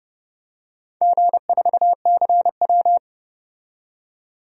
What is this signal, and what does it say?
Morse code, heard as a keyed tone, about 30 wpm, 700 Hz. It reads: G4CW
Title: DXCC-100 Call Signs - 4 Characters - 30wpm